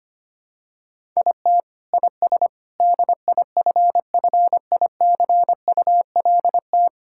Morse code, heard as a keyed tone, about 25 wpm, 700 Hz. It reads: IT IS DIFFICULT
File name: practice / 25wpm / Sentences with 4-Word Phrases - 25wpm